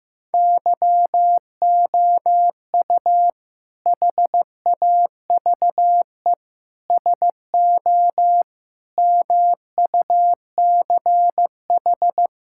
Morse code, heard as a keyed tone, about 15 wpm, 700 Hz. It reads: YOU HAVE SO MUCH
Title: Sentences from Top 1600 Words 15wpm